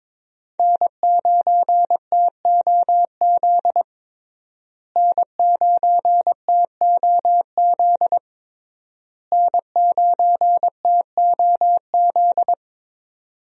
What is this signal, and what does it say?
Morse code, heard as a keyed tone, about 21 wpm, 700 Hz. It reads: N9TOZ N9TOZ N9TOZ
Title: USA Call Signs - 5 Characters - Repeated 3x - 22wpm